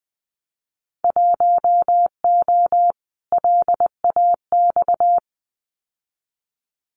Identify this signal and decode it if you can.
Morse code, heard as a keyed tone, about 20 wpm, 700 Hz. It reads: 1O LAX
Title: Winter Field Day Exchanges 20wpm